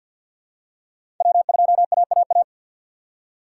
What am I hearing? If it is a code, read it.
Morse code, heard as a keyed tone, about 52 wpm, 700 Hz. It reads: W2AAA
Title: USA Call Signs - 5 Characters - 2x Character Spacing - 50wpm